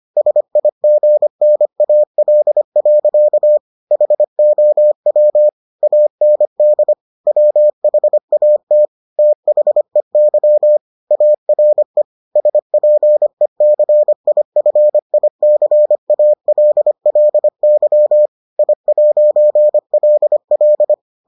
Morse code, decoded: SIGNAL. HOW AND WHAT THEY ARE SPECIFICALLY I'LL

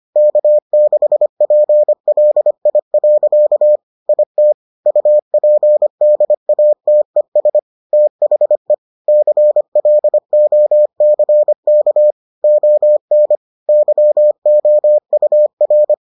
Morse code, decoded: K6PLI. IT UPDATES THE CLOCK ON YOUR

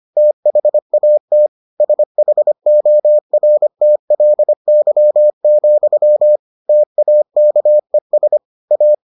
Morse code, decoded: THAT SHORTLY, TAKES A

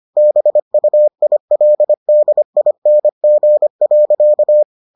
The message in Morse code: BUILDING.

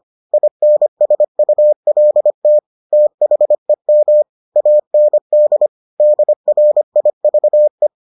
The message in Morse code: INSULT THEM AND DRIVE

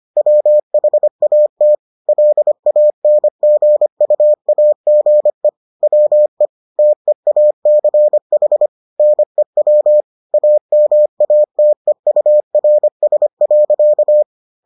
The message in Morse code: WHAT LANGUAGE WE TEACH NEW AMATEURS.

